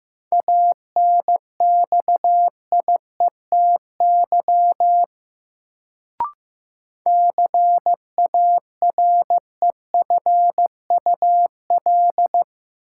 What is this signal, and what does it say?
Morse code, heard as a keyed tone, about 15 wpm, 700 Hz. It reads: ANXIETY CAREFUL
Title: Sentences from Top 1600 Words - New Word Drill Alt - 15wpm